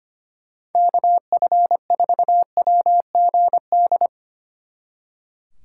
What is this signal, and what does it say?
Morse code, heard as a keyed tone, about 31 wpm, 700 Hz. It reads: KF4WGD
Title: Speed Racing - USA 2x3 Call Signs 25wpm